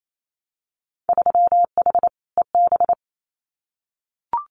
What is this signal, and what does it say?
Morse code, heard as a keyed tone, about 28 wpm, 700 Hz. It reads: 3H EB
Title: Winter Field Day Exchanges Encore 28wpm